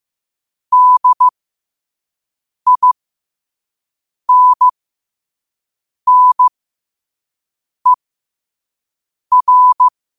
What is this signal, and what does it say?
Morse code, heard as a keyed tone, about 15 wpm, spaced as for 5 wpm, 1000 Hz. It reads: DINNER